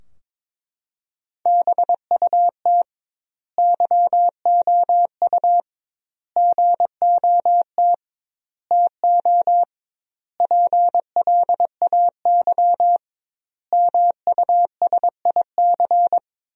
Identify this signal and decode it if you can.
Morse code, heard as a keyed tone, about 22 wpm, 700 Hz. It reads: BUT YOU GOT TO PLAY MUSIC
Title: 2x Word-Spacing - Sentences from Top 300 Words - 22wpm